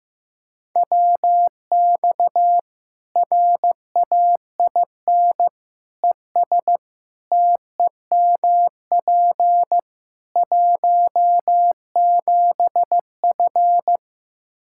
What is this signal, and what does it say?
Morse code, heard as a keyed tone, about 15 wpm, 700 Hz. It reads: WX RAIN ES TEMP 17F
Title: Practice QSO - Exchange 2 - Info - 15wpm